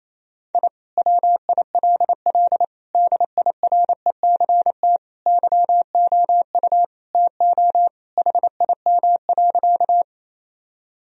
Morse code, heard as a keyed tone, about 28 wpm, 700 Hz. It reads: I WILL DIRECT YOU TO HIM.
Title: Sentences from Top 1000 Words - Review Part 2 - 28wpm